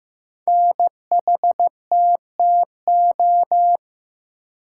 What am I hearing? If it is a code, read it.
Morse code, heard as a keyed tone, about 15 wpm, 700 Hz. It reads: NHTTO
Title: Code Groups - 5 Letters - 15wpm